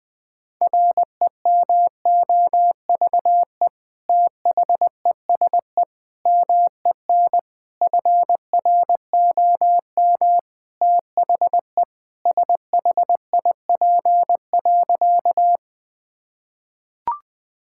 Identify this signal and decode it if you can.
Morse code, heard as a keyed tone, about 20 wpm, 700 Hz. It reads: REMOVE THESE MEN FROM THE SHIP.
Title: Sentences from Top 1000 Words - Review Part 5 - 20wpm